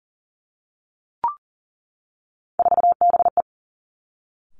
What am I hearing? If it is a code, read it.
Morse code, heard as a keyed tone, about 42 wpm, 700 Hz. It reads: 4BE
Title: Three Challenging Letter-Number Combinations 40wpm